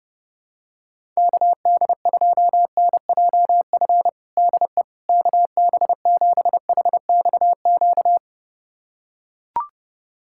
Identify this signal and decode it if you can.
Morse code, heard as a keyed tone, about 32 wpm, 700 Hz. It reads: KD2NJF DE KB7HXQ
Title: USA Callsign DE Callsign - 6 Characters - 30wpm